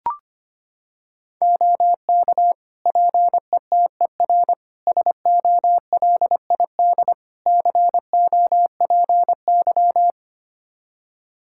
Morse code, heard as a keyed tone, about 25 wpm, 700 Hz. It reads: OK PETER SOLID COPY
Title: K1USN SST QSO Practice 25wpm